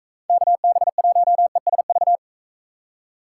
Morse code, heard as a keyed tone, about 42 wpm, 700 Hz. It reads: KB1ESV